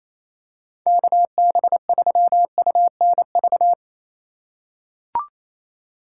Morse code, heard as a keyed tone, about 28 wpm, 700 Hz. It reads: KB3UNV
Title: USA Call Signs - 6 Characters - 3x Character Spacing - 28wpm